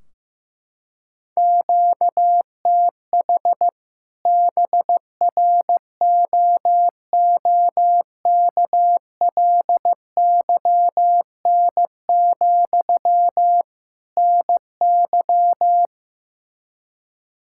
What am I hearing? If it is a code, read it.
Morse code, heard as a keyed tone, about 15 wpm, 700 Hz. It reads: QTH BROOKLYN, NY
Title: Practice QSO - Exchange 1 - Beginning - 15wpm